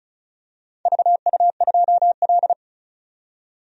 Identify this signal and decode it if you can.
Morse code, heard as a keyed tone, about 36 wpm, 700 Hz. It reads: VU2L
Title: DXCC-100 Call Signs - 4 Characters - 35wpm